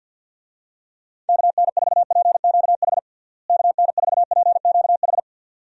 Morse code, heard as a keyed tone, about 47 wpm, 700 Hz. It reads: KN4PQH KN4PQH
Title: USA Call Signs - 6 Characters - Repeated 2x - 50wpm